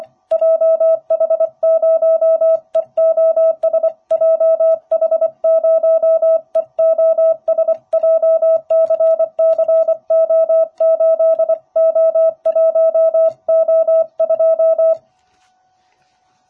Morse code, heard as a keyed tone, about 25 wpm, 650 Hz. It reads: JH0EOS JH0EOS JCCO8O1O2